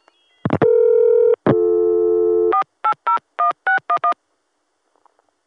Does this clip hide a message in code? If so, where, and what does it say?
Yes: DTMF from 2.5 to 4.1 s, heard as telephone touch tones, tones of about 100 ms. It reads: 4801611